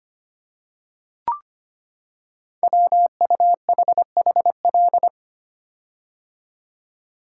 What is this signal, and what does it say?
Morse code, heard as a keyed tone, about 25 wpm, 700 Hz. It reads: WUHHL
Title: Code Groups - 5 Letters - 25wpm